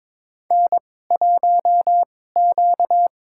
Morse code, heard as a keyed tone, about 22 wpm, spaced as for 16 wpm, 700 Hz. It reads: N1Q
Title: USA Call Signs - 2x Character Spacing - 22wpm